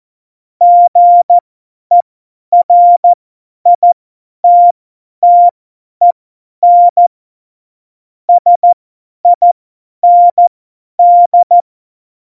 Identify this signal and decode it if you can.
Morse code, heard as a keyed tone, about 14 wpm, spaced as for 10 wpm, 700 Hz. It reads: GERITTEN SIND
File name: Morsekurs / Witze / Witz 18 August 2020, 50 F70